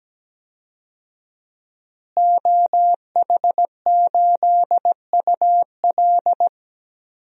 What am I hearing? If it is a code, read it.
Morse code, heard as a keyed tone, about 17 wpm, 700 Hz. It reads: OH8UL
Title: DXCC-100 Call Signs - 5 Characters - Repeated 2x - 17wpm